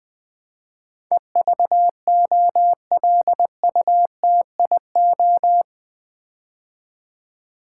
Morse code, heard as a keyed tone, about 20 wpm, 700 Hz. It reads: EVOLUTIO
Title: Build A Word - 4 Letters At A Time - 12-Letter Words - 20wpm